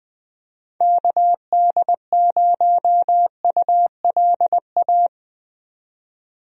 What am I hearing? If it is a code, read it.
Morse code, heard as a keyed tone, about 20 wpm, 700 Hz. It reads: KD0ULA
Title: USA Call Signs - 6 Characters - 20wpm